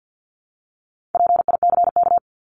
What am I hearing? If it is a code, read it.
Morse code, heard as a keyed tone, about 50 wpm, 700 Hz. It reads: PICK
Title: Top 751-1000 Words 50wpm